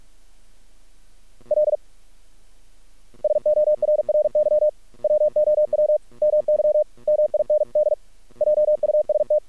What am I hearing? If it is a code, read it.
Morse code, heard as a keyed tone, about 48 wpm, 610 Hz. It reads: L SORRY WOW MY MIND JUST
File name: archived / CRSnet